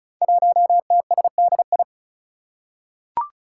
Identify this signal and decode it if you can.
Morse code, heard as a keyed tone, about 36 wpm, 700 Hz. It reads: 1TSDI
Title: Five Challenging Letter-Number Combinations 35wpm